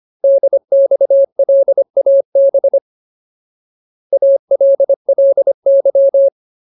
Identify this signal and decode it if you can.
Morse code, heard as a keyed tone, about 25 wpm, 550 Hz. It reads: DXLAB ALLY
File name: Ultimate Word List / 25WPM Ultimate